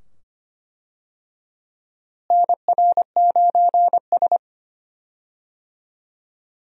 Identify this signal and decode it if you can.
Morse code, heard as a keyed tone, about 25 wpm, 700 Hz. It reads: NR9S
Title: USA Call Signs - 4 Characters - Repeated 2x - 25wpm